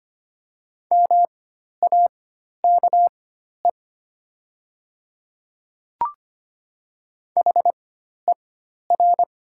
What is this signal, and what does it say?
Morse code, heard as a keyed tone, about 25 wpm, spaced as for 12 wpm, 700 Hz. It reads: MAKE HER
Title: 4x Character Spacing - Top 100 Words - 25wpm